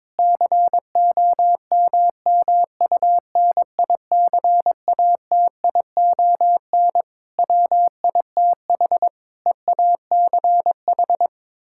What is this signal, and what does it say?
Morse code, heard as a keyed tone, about 22 wpm, 700 Hz. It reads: COMMUNICATION WITH EACH
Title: Sentences from Top 1000 Words - Review Part 6 - 22wpm